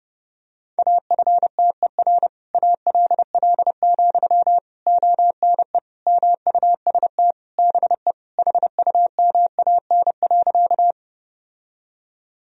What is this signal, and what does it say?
Morse code, heard as a keyed tone, about 30 wpm, 700 Hz. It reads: AFTER ALL, ONE MUST BE HUMAN.
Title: Sentences from Top 1000 Words - Review Part 3 - 30wpm